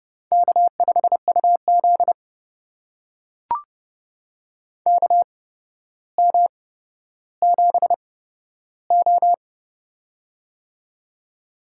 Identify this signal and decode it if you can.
Morse code, heard as a keyed tone, about 28 wpm, spaced as for 13 wpm, 700 Hz. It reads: K5UZ KM7O
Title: USA Call Signs - 4 Characters - 8x Character Spacing - 30wpm